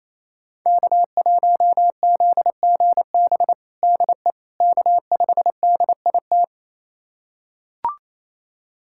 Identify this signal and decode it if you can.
Morse code, heard as a keyed tone, about 28 wpm, 700 Hz. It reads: K1ZGB DE K5DIT
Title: USA Callsign DE Callsign - 5 Characters - 28wpm